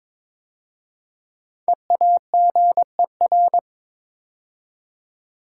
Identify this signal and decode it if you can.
Morse code, heard as a keyed tone, about 23 wpm, 700 Hz. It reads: EAGER
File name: 5 Letter Words - First 3 Letters Given - 22wpm